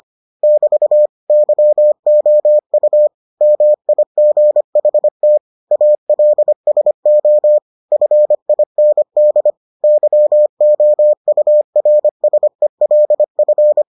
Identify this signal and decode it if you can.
Morse code, heard as a keyed tone, about 25 wpm, 600 Hz. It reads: <BT> YOU MIGHT ALSO FIND YOURSELF